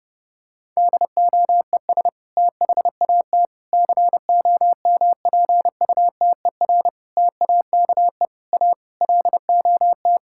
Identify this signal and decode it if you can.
Morse code, heard as a keyed tone, about 30 wpm, 700 Hz. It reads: DOES THAT COMPUTER TAKE A LOT
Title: Sentences from Top 1000 Words - Review Part 6 - 30wpm